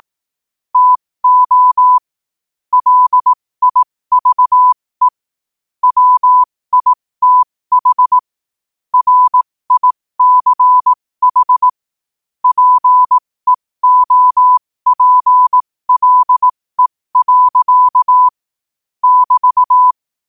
Morse code, heard as a keyed tone, about 18 wpm, spaced as for 15 wpm, 1000 Hz. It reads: TO LIVE WITH RICH PEOPLE. <BT>
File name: qotdfiles / 15wpm